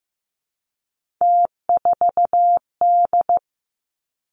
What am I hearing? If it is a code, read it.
Morse code, heard as a keyed tone, about 15 wpm, 700 Hz. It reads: T4D
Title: Three Challenging Letter-Number Combinations 15wpm